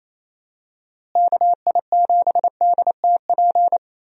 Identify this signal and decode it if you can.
Morse code, heard as a keyed tone, about 28 wpm, 700 Hz. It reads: KI7DTP
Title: USA Call Signs - 6 Characters - Repeated 3x - 28wpm